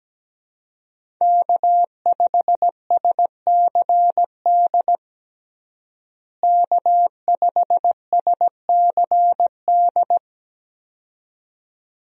Morse code, heard as a keyed tone, about 17 wpm, 700 Hz. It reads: K5SCD K5SCD
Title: USA Call Signs - 5 Characters - Repeated 2x - 17wpm